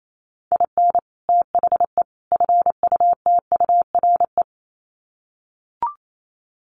Morse code, heard as a keyed tone, about 28 wpm, 700 Hz. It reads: IN THE FUTURE
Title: Sets of 3 Words 28wpm